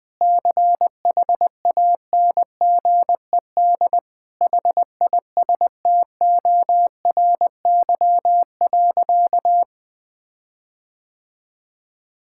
Morse code, heard as a keyed tone, about 21 wpm, 700 Hz. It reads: CHANGED HISTORY.
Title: Sentences with 7-Word Phrases - Spoken Beginning - 20wpm